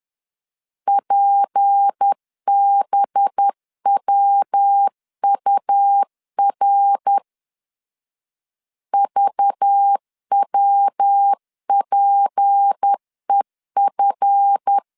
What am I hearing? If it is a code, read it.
Morse code, heard as a keyed tone, about 10 wpm, 800 Hz. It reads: PBWUR VWPEF
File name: CW / DET / 1 9 1500C S B 10wpm 4439